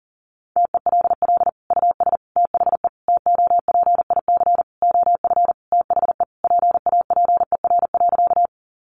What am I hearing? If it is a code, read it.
Morse code, heard as a keyed tone, about 40 wpm, 700 Hz. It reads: TELL US THE TOPIC OF THE PAPER.
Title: Sentences from Top 1200 Words 40wpm